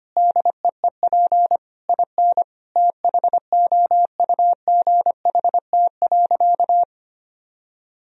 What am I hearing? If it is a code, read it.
Morse code, heard as a keyed tone, about 25 wpm, 700 Hz. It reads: DEEP IN THOUGHT.
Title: Sentences from Top 1000 Words - Review Part 3 - 25wpm